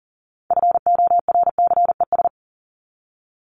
Morse code, heard as a keyed tone, about 40 wpm, 700 Hz. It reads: FORCES
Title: ES:S Suffix - Top 100 Words with Base Words 40wpm